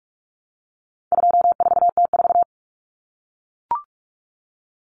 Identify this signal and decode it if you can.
Morse code, heard as a keyed tone, about 44 wpm, 700 Hz. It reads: 24T4
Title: Common Cut Numbers - 4-Digit 45wpm